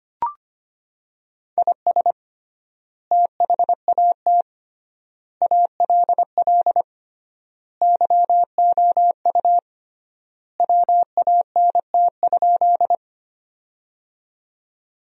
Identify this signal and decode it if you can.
Morse code, heard as a keyed tone, about 25 wpm, 700 Hz. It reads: IS THAT ALL YOU WANT?
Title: 3x Word-Spacing - Sentences from Top 200 Words - 25wpm